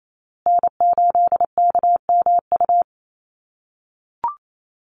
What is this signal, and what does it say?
Morse code, heard as a keyed tone, about 28 wpm, 700 Hz. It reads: N8KMU
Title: USA Call Signs Repeated 2x 28wpm